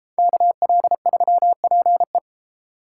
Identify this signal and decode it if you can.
Morse code, heard as a keyed tone, about 36 wpm, 700 Hz. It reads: KL3PE